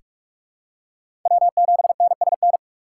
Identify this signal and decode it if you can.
Morse code, heard as a keyed tone, about 44 wpm, 690 Hz. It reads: W7NSN
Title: USA Call Signs - 5 Characters - 4x Character Spacing - 45wpm